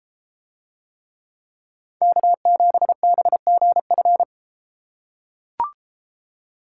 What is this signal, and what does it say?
Morse code, heard as a keyed tone, about 32 wpm, 700 Hz. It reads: K7BGF